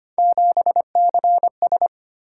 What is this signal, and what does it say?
Morse code, heard as a keyed tone, about 31 wpm, 700 Hz. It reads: 7CS
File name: Speed-Racing - Three Letter-Number Combinations - 25wpm